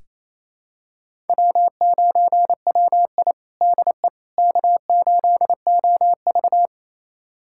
Morse code, heard as a keyed tone, about 28 wpm, 700 Hz. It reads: W9WI DE K8OV
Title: USA Callsign DE Callsign - 4 Characters - 28wpm